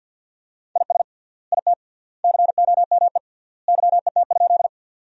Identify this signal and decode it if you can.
Morse code, heard as a keyed tone, about 53 wpm, 700 Hz. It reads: IS IT COME YET?
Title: 3x Word-Spacing - Sentences from Top 500 Words - 50wpm